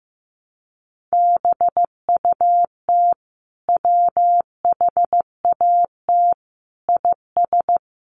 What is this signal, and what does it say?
Morse code, heard as a keyed tone, about 15 wpm, 700 Hz. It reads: BUT WHAT IS
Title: Sentences from Top 100 Words 15wpm